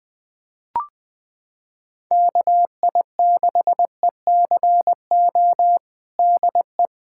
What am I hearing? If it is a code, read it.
Morse code, heard as a keyed tone, about 21 wpm, 700 Hz. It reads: KI6ECO DE
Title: USA Callsign DE Callsign - 6 Characters - 20wpm